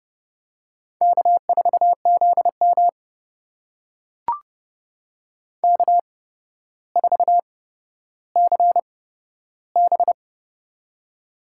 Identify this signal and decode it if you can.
Morse code, heard as a keyed tone, about 30 wpm, spaced as for 13 wpm, 700 Hz. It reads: K4ZM K4CB
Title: USA Call Signs - 4 Characters - 8x Character Spacing - 30wpm